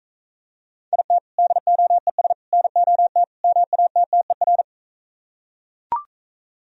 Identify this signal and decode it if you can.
Morse code, heard as a keyed tone, about 43 wpm, 700 Hz. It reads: IT DOES NOT MATTER